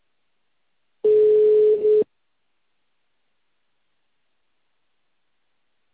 Ringback tone (Europe, 425 Hz).